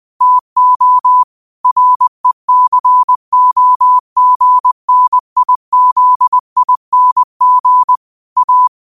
Morse code, heard as a keyed tone, about 20 wpm, 1000 Hz. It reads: TO RECOGNIZING A
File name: qotdfiles / 20wpm